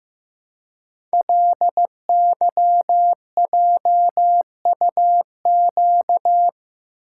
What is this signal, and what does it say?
Morse code, heard as a keyed tone, about 15 wpm, 700 Hz. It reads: LYJUQ
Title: Code Groups - 5 Letters - 15wpm